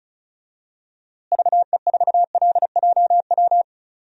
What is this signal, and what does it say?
Morse code, heard as a keyed tone, about 36 wpm, 700 Hz. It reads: VE4LJW
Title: DXCC-100 Call Signs - 6 Characters - Repeated 2x - 35wpm